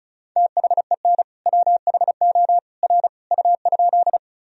Morse code, heard as a keyed tone, about 34 wpm, 700 Hz. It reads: THEN WHO R U?
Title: Sentences from Top 100 Words and Abbreviations 35wpm